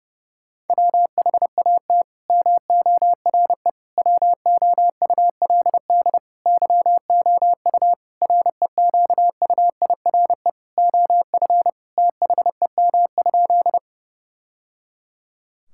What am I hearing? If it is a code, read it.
Morse code, heard as a keyed tone, about 30 wpm, 700 Hz. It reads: WHAT MORE WOULD YOU REQUIRE OF THEM?
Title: Sentences from Top 1000 Words - Review Part 6 - 30wpm